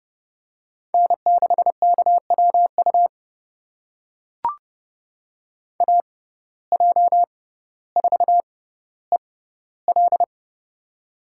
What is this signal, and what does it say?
Morse code, heard as a keyed tone, about 30 wpm, spaced as for 15 wpm, 700 Hz. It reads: N6KWU AJ4EL
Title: USA Call Signs - 5 Characters - 6x Character Spacing - 30wpm